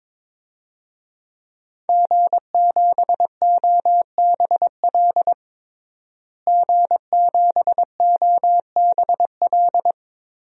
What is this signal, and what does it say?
Morse code, heard as a keyed tone, about 22 wpm, 700 Hz. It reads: G7OBL G7OBL
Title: DXCC-100 Call Signs - 5 Characters - Repeated 3x - 22wpm